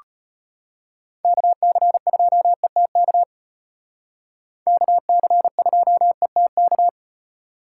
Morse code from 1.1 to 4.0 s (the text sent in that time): KC2ETK